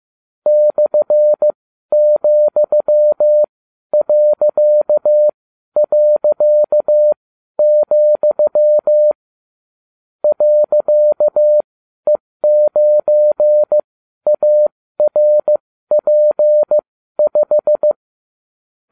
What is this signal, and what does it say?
Morse code, heard as a keyed tone, about 15 wpm, spaced as for 12 wpm, 600 Hz. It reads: /,.., .E9ARP5